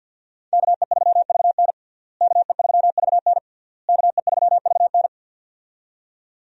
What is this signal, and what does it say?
Morse code, heard as a keyed tone, about 49 wpm, 700 Hz. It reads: KE3VN KE3VN KE3VN
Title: USA Call Signs - 5 Characters - Repeated 3x - 50wpm